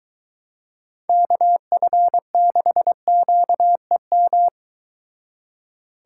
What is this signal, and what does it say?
Morse code, heard as a keyed tone, about 22 wpm, 700 Hz. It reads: KF6QEM